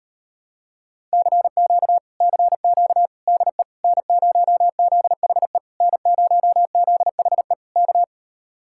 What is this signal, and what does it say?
Morse code, heard as a keyed tone, about 38 wpm, 700 Hz. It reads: CQ CQ DE N0ZHE N0ZHE K